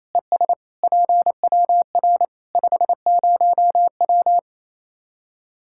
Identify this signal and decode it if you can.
Morse code, heard as a keyed tone, about 28 wpm, 700 Hz. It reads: ES PWR 50W
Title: Practice QSO - Exchange 2 - Info - 28wpm